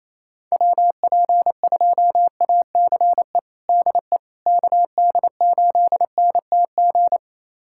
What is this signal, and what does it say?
Morse code, heard as a keyed tone, about 27 wpm, 700 Hz. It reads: WP2ACE DE KD8NTG
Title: USA Callsign DE Callsign - 6 Characters - 28wpm